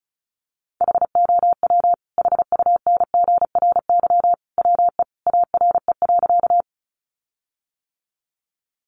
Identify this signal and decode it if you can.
Morse code, heard as a keyed tone, about 34 wpm, 700 Hz. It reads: HOW HUNGRY WE ARE.
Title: Sentences from Top 1400 Words 35wpm